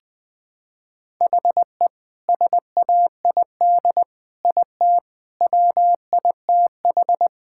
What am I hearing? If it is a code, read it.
Morse code, heard as a keyed tone, about 20 wpm, 700 Hz. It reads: HE SAID IT WITH
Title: Sentences from Top 1600 Words 20wpm